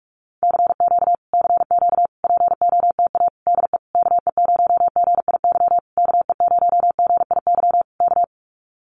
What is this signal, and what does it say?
Morse code, heard as a keyed tone, about 46 wpm, 700 Hz. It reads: CQ CQ POTA DE KE0GIY KE0GIY K
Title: Common QSO Phrases Encore 45wpm